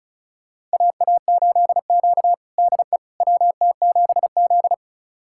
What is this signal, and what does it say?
Morse code, heard as a keyed tone, about 36 wpm, 700 Hz. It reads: AA8Q DE WT7Z
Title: USA Callsign DE Callsign - 4 Characters - 35wpm